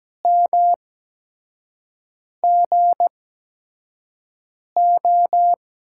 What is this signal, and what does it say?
Morse code, heard as a keyed tone, about 17 wpm, spaced as for 5 wpm, 700 Hz. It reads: MGO
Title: USA Call Signs - 8x Character Spacing - 17wpm